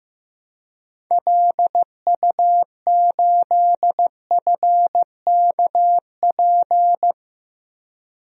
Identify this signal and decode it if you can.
Morse code, heard as a keyed tone, about 15 wpm, 700 Hz. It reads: LU8FKP
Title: DXCC-100 Call Signs - 6 Characters - Repeated 3x - 15wpm